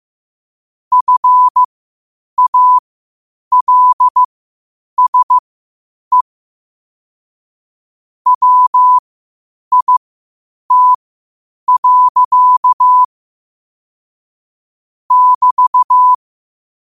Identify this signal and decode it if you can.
Morse code, heard as a keyed tone, about 15 wpm, spaced as for 8 wpm, 1000 Hz. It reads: FALSE WIT. <BT>
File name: qotdfiles / 8wpm